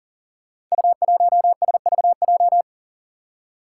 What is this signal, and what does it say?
Morse code, heard as a keyed tone, about 38 wpm, 700 Hz. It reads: U1SVJ
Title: Five Challenging Letter-Number Combinations 40wpm